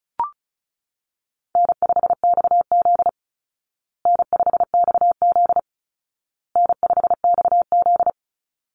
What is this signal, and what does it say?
Morse code, heard as a keyed tone, about 34 wpm, 700 Hz. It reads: N5XZ N5XZ N5XZ
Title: Easy Call Signs Repeated 3x - 35wpm